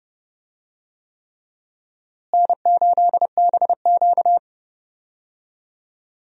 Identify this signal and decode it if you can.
Morse code, heard as a keyed tone, about 30 wpm, 700 Hz. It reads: N8BQ
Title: USA Call Signs - 4 Characters - Repeated 3x - 30wpm